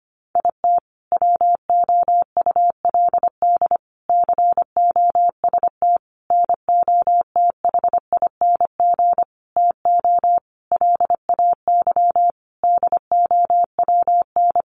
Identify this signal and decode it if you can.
Morse code, heard as a keyed tone, about 25 wpm, 700 Hz. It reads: IT WOULD COST NOTHING TO LAY DOWN
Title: Sentences from Top 800 Words 25wpm